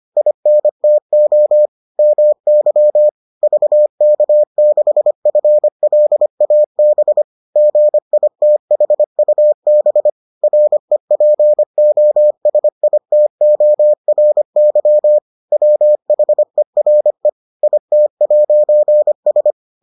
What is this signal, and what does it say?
Morse code, heard as a keyed tone, about 25 wpm, 600 Hz. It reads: INTO MY VK6FLAB GITHUB REPOSITORY WHERE IT'S